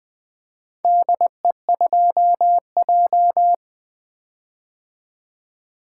Morse code, heard as a keyed tone, about 20 wpm, 700 Hz. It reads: DE2J
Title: Four Letter-Number Combinations - 20wpm